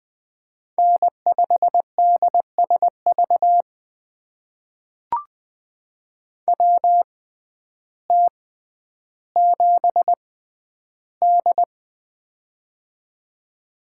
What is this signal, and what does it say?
Morse code, heard as a keyed tone, about 20 wpm, spaced as for 11 wpm, 700 Hz. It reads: N5DSV WT7D
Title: USA Call Signs - 6x Character Spacing - 20wpm